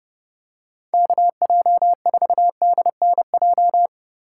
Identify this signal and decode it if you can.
Morse code, heard as a keyed tone, about 30 wpm, 700 Hz. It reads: KJ4DNJ